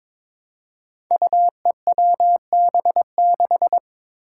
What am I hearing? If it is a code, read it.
Morse code, heard as a keyed tone, about 23 wpm, 700 Hz. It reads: UEWB6
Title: Five Challenging Letter-Number Combinations 22wpm